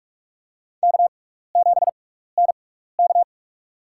Morse code, heard as a keyed tone, about 46 wpm, spaced as for 16 wpm, 700 Hz. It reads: K7NK
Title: USA Call Signs - 4 Characters - 6x Character Spacing - 45wpm